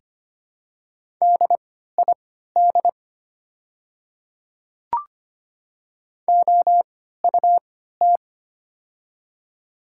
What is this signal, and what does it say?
Morse code, heard as a keyed tone, about 25 wpm, spaced as for 14 wpm, 690 Hz. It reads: DID OUT
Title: 3x Character Spacing - Top 100 Words - 25wpm